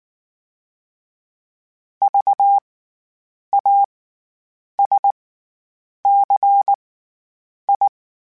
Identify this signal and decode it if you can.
Morse code, heard as a keyed tone, about 19 wpm, spaced as for 8 wpm, 800 Hz. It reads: VASCI
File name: Special INGs / Special INGs-mp3s / SlowToFaster ing @18